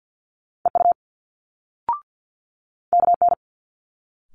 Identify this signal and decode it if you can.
Morse code, heard as a keyed tone, about 50 wpm, 700 Hz. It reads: EU KN
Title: Multi-letter Phonemes Quick Fire 50wpm